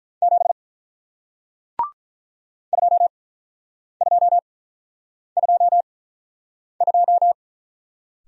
Morse code, heard as a keyed tone, about 48 wpm, spaced as for 9 wpm, 700 Hz. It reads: 7 2222